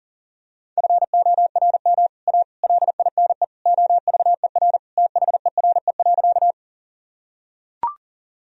Morse code, heard as a keyed tone, about 40 wpm, 700 Hz. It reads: FORM A LINE OVER THERE.
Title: Sentences from Top 1000 Words - Review Part 1 - 40wpm